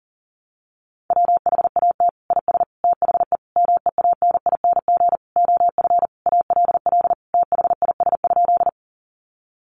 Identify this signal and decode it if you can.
Morse code, heard as a keyed tone, about 40 wpm, 700 Hz. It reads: WHAT IS THE MEANING OF ALL THIS?
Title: Sentences from Top 900 Words 40wpm